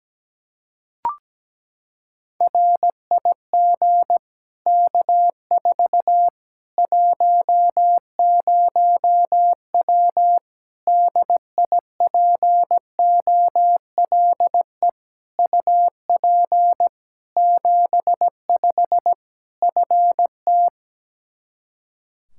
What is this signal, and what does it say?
Morse code, heard as a keyed tone, about 17 wpm, 700 Hz. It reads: RIG K4 10W DIPOLE UP 75 FT
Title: Practice QSO - Exchange 2 - Info - 17wpm